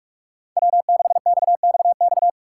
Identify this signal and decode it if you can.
Morse code, heard as a keyed tone, about 46 wpm, 700 Hz. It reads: W6XXX
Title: USA Call Signs - 5 Characters - 3x Character Spacing - 45wpm